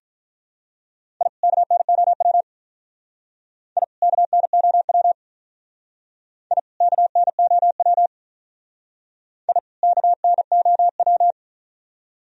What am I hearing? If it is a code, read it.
Morse code, heard as a keyed tone, about 44 wpm, 700 Hz. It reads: I KNOW I KNOW I KNOW I KNOW
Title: Speed-Racing - Sets of 2 Words 35wpm